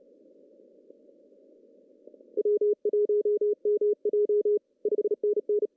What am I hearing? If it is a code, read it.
Morse code, heard as a keyed tone, about 30 wpm, 410 Hz. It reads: W1MJ 5NN
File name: CW / cqww-cw-k5tr